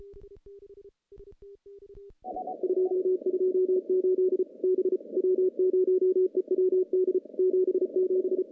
Morse code, heard as a keyed tone, about 34 wpm, 370 Hz. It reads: 228 B W0EWD 77
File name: contests / CW-SS-k5tr- -0000